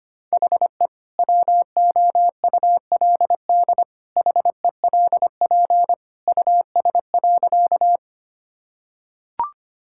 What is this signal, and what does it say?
Morse code, heard as a keyed tone, about 25 wpm, 700 Hz. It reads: HE WOULD HELP US.
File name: practice / 25wpm / Sentences with 2-Word Phrases - 25wpm